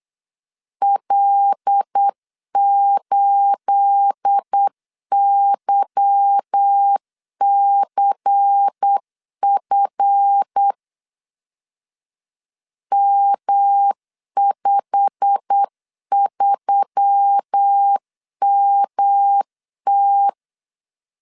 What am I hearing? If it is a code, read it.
Morse code, heard as a keyed tone, about 8 wpm, 800 Hz. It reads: L8YCF M53MT
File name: LEZIONI / 9 28 8wpm 3512